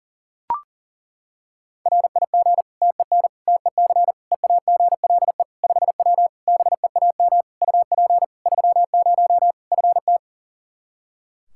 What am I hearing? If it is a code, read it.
Morse code, heard as a keyed tone, about 40 wpm, 700 Hz. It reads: RIG TEN TEC EAGLE 5W BEAM UP 30 FT
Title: Practice QSO - Exchange 2 - Info - 40wpm